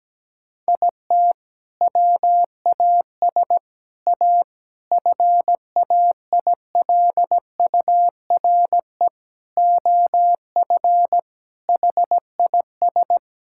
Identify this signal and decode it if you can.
Morse code, heard as a keyed tone, about 17 wpm, 700 Hz. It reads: IT WAS A FAILURE OF HIS
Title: Sentences from Top 1000 Words - Review Part 6 - 17wpm